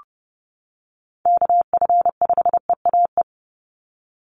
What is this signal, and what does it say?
Morse code, heard as a keyed tone, about 30 wpm, 700 Hz. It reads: KF5EAE
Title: USA Call Signs 30wpm